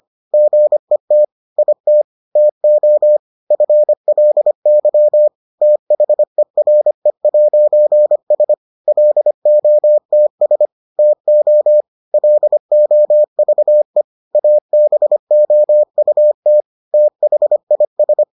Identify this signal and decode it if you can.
Morse code, heard as a keyed tone, about 25 wpm, 600 Hz. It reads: GET IT TO FLY THERE'S LOTS TO LOVE ABOUT THIS